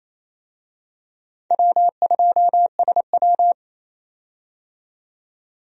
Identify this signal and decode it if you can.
Morse code, heard as a keyed tone, about 28 wpm, 700 Hz. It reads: W2SW
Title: USA Call Signs - 4 Characters - 28wpm